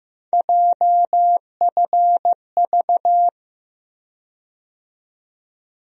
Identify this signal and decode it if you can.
Morse code, heard as a keyed tone, about 15 wpm, 700 Hz. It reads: JFV
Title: USA Call Sign Postfixes - 3 Letters 15wpm